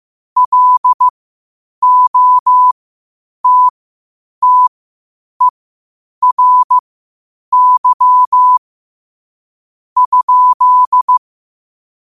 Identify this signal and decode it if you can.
Morse code, heard as a keyed tone, about 15 wpm, spaced as for 8 wpm, 1000 Hz. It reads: LOTTERY?
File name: qotdfiles / 8wpm